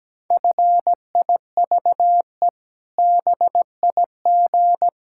Morse code, heard as a keyed tone, about 17 wpm, 700 Hz. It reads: FIVE BIG
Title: Pangram Challenge v2 - 17wpm